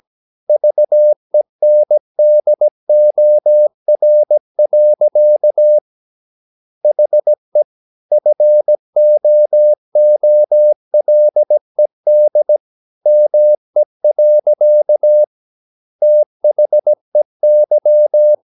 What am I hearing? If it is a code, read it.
Morse code, heard as a keyed tone, about 17 wpm, 600 Hz. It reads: VENDOR. HE FOOLED ME. THEY
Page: Romanen Never af Ken Follet på engelsk.